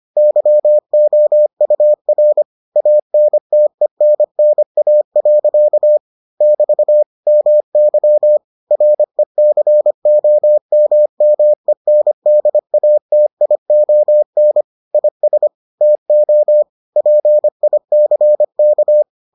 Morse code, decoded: YOUR ANTENNA. <BT> MY RECOMMENDATION IS TO PICK

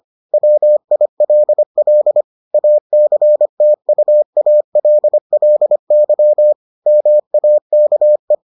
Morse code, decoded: WILL ACTUALLY MAKE

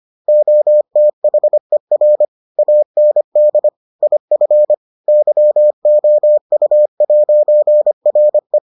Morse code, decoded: OTHER AND IF YOU'RE

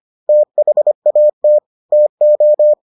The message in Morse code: THAT TO

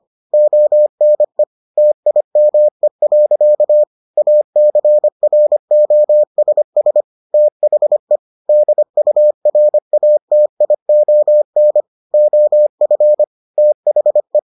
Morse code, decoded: ONE TIME. ACROSS THE DURATION OF THE

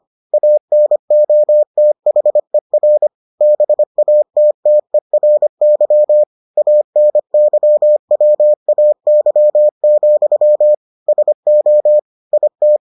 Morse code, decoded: ANOTHER BATTERY ANYWAY, SO IT

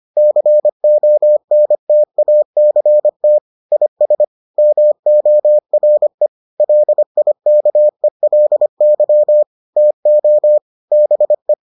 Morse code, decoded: CONTACT IS MORE LIKELY TO BE

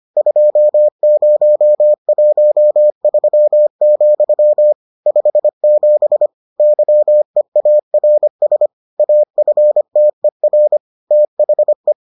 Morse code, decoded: 2013, 57 YEARS AFTER THE

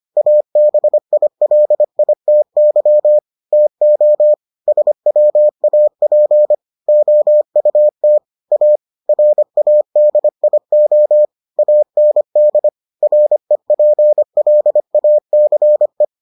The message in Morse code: ABILITY TO SWAP OUT A RADIO AND REPLACE

